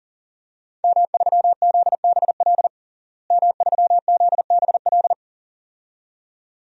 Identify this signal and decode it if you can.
Morse code, heard as a keyed tone, about 40 wpm, 700 Hz. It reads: M3ZBL M3ZBL
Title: DXCC-100 Call Signs - 5 Characters - Repeated 2x - 40wpm